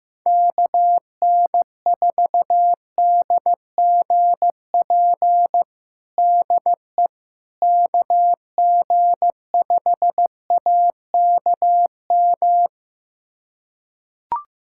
Morse code, heard as a keyed tone, about 15 wpm, 700 Hz. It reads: KN4DGP DE KG5AKM
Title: USA Callsign DE Callsign - 6 Characters - 15wpm